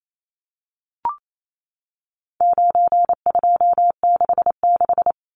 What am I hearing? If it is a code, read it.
Morse code, heard as a keyed tone, about 28 wpm, 700 Hz. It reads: 9266
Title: Numbers - 4-Digit 28wpm